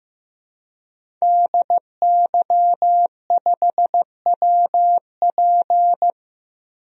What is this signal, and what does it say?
Morse code, heard as a keyed tone, about 15 wpm, 700 Hz. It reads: DY5WP
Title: DXCC-100 Call Signs - 5 Characters - Repeated 3x - 15wpm